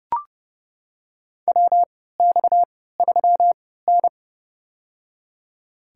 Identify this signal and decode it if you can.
Morse code, heard as a keyed tone, about 32 wpm, spaced as for 17 wpm, 700 Hz. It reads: WX3N
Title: USA Call Signs - 4 Characters - 3x Character Spacing - 30wpm